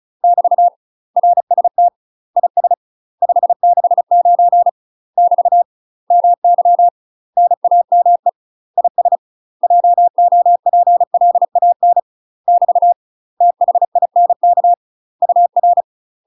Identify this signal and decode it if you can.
Morse code, heard as a keyed tone, about 34 wpm, 700 Hz. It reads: <BT> RST IS 569 <BT> MY NAME IS JOPLAN <BT> THINK UR
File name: UK MASTER QSOs TRAINING/35WPM